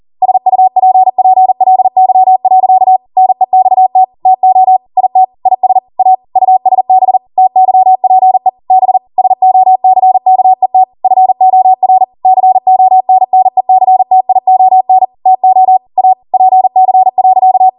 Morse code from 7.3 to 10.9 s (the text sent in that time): TYPE B SOCKET